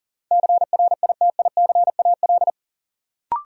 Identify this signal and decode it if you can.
Morse code, heard as a keyed tone, about 40 wpm, 700 Hz. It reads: CRITICAL
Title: Sentences from Top 1100 Words - New Word Drill - 40wpm